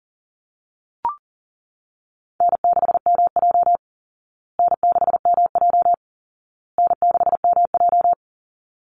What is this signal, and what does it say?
Morse code, heard as a keyed tone, about 40 wpm, 700 Hz. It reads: N6MJ N6MJ N6MJ
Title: Easy Call Signs Repeated 3x - 40wpm